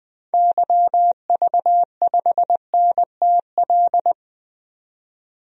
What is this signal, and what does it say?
Morse code, heard as a keyed tone, about 21 wpm, 700 Hz. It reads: YV5NTL
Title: DXCC-100 Call Signs - 6 Characters - 20wpm